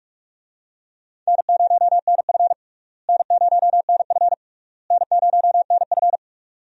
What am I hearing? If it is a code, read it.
Morse code, heard as a keyed tone, about 42 wpm, 700 Hz. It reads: N0NF N0NF N0NF
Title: USA Call Signs - 4 Characters - Repeated 3x - 45wpm